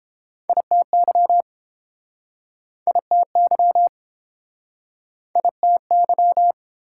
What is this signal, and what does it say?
Morse code, heard as a keyed tone, about 29 wpm, 700 Hz. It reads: ITY ITY ITY